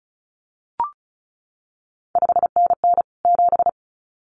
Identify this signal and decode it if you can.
Morse code, heard as a keyed tone, about 36 wpm, 700 Hz. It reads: 5NN 7
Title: CQ WW DX Contest Exchange - 35wpm